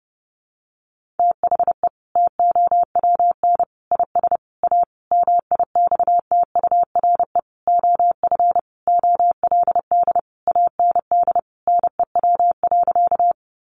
Morse code, heard as a keyed tone, about 28 wpm, 700 Hz. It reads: THE TOWN IS A MIXTURE OF OLD AND NEW.
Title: Sentences from Top 1500 Words 30wpm